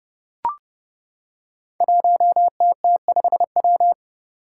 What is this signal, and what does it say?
Morse code, heard as a keyed tone, about 30 wpm, 700 Hz. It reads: 1TT5W
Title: Five Challenging Letter-Number Combinations 30wpm